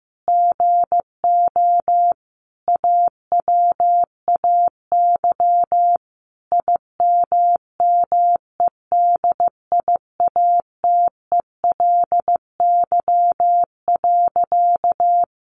Morse code, heard as a keyed tone, about 15 wpm, 700 Hz. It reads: GO AWAY IMMEDIATELY.
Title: Sentences from Top 1100 Words 15wpm